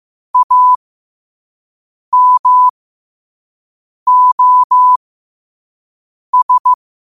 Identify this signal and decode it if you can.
Morse code, heard as a keyed tone, about 15 wpm, spaced as for 5 wpm, 1000 Hz. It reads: AMOS
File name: qotdfiles / 5wpm